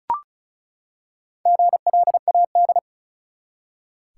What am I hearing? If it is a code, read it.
Morse code, heard as a keyed tone, about 36 wpm, 700 Hz. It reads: GLAD
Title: Four Letter Words - 35wpm